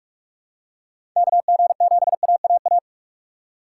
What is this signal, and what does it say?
Morse code, heard as a keyed tone, about 46 wpm, 700 Hz. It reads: KG7AAA
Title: USA Call Signs - 6 Characters - 2x Character Spacing - 45wpm